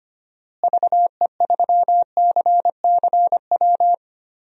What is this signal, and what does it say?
Morse code, heard as a keyed tone, about 25 wpm, 700 Hz. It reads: VE3CCW
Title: DXCC-100 Call Signs - 6 Characters - Repeated 3x - 25wpm